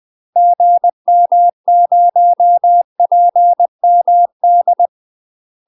Morse code, heard as a keyed tone, about 20 wpm, 700 Hz. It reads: GM0PMD
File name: UK MASTER QSOs TRAINING / 20WPM / UK MASTER QSOs 0022